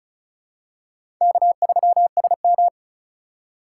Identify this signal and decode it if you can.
Morse code, heard as a keyed tone, about 36 wpm, 700 Hz. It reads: K3SM
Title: USA Call Signs - 8x Character Spacing - 35wpm